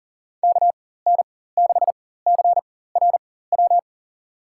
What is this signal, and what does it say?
Morse code, heard as a keyed tone, about 40 wpm, spaced as for 19 wpm, 700 Hz. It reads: KN6CRW
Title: USA Call Signs - 6 Characters - 4x Character Spacing - 40wpm